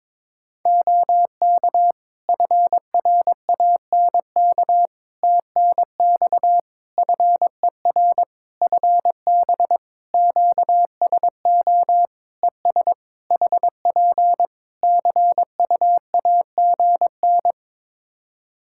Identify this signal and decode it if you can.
Morse code, heard as a keyed tone, about 22 wpm, 700 Hz. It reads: OK FRANK TNX FER FB QSO ES HP CUAGN
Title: Practice QSO - Exchange 4 - Ending - 22wpm